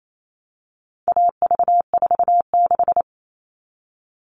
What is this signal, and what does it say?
Morse code, heard as a keyed tone, about 28 wpm, 700 Hz. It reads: AV46
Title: All Cut Numbers - 4-Digit 28wpm